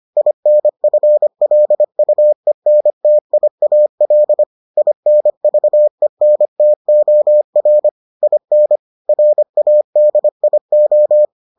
Morse code, heard as a keyed tone, about 25 wpm, 600 Hz. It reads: INFLUENTIAL INVENTOR IN RADIO